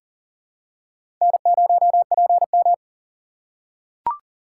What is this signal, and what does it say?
Morse code, heard as a keyed tone, about 40 wpm, 700 Hz. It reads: N0PM
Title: USA Call Signs - 4x Character Spacing - 40wpm